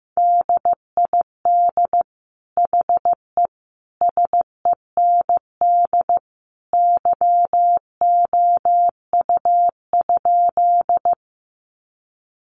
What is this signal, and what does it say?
Morse code, heard as a keyed tone, about 15 wpm, 700 Hz. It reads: DID HE SEND YOU?
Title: Sentences from Top 1500 Words 15wpm